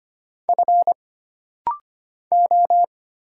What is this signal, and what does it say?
Morse code, heard as a keyed tone, about 25 wpm, 700 Hz. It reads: F O
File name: Single Character (Letter:Number:Punctuation:Prosign) - Rapid-Fire - 25wpm